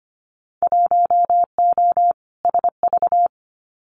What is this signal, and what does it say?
Morse code, heard as a keyed tone, about 25 wpm, 700 Hz. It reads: 1O SV
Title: Winter Field Day Exchanges 25wpm